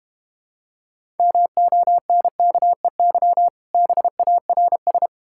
Morse code, heard as a keyed tone, about 30 wpm, 700 Hz. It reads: MONKEY BARS
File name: Easy Words - Speed-Racing with Hints - 25wpm